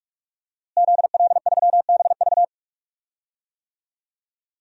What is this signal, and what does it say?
Morse code, heard as a keyed tone, about 46 wpm, 700 Hz. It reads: ZL3BV
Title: DXCC-100 Call Signs - 5 Characters - Repeated 3x - 45wpm